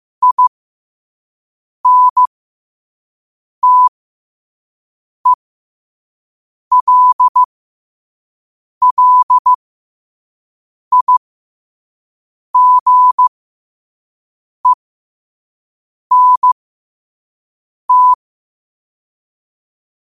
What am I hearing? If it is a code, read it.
Morse code, heard as a keyed tone, about 15 wpm, spaced as for 5 wpm, 1000 Hz. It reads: INTELLIGENT
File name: qotdfiles / 5wpm